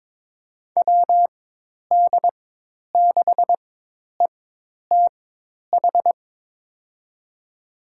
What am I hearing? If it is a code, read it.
Morse code, heard as a keyed tone, about 22 wpm, spaced as for 10 wpm, 700 Hz. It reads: WD6ETH
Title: USA Call Signs - 6 Characters - 4x Character Spacing - 22wpm